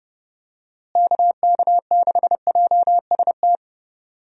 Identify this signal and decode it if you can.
Morse code, heard as a keyed tone, about 30 wpm, 700 Hz. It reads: KK6JST
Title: USA Call Signs - 6 Characters - Repeated 2x - 30wpm